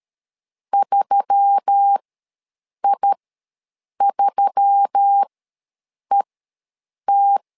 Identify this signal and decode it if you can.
Morse code, heard as a keyed tone, about 13 wpm, spaced as for 7 wpm, 800 Hz. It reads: 3I3ET
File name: LEZIONI / 4 Q1S3 ws 12wpm 2038